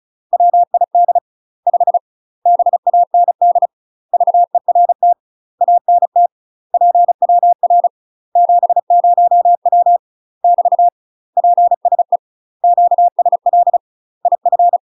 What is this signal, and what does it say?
Morse code, heard as a keyed tone, about 36 wpm, 700 Hz. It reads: WID 5 BAND VERT ANT PWR 70W <BT> PSE QSL IF
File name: UK MASTER QSOs TRAINING / 35WPM / UK MASTER QSOs 0036